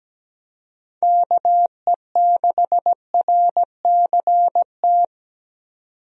Morse code, heard as a keyed tone, about 17 wpm, 700 Hz. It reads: KE6RCT
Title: USA Call Signs - 6 Characters - 17wpm